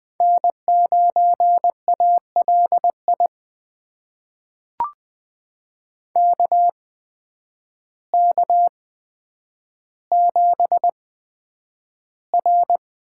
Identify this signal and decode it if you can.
Morse code, heard as a keyed tone, about 20 wpm, spaced as for 9 wpm, 700 Hz. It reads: N9ALI KK7R
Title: USA Call Signs - 8x Character Spacing - 20wpm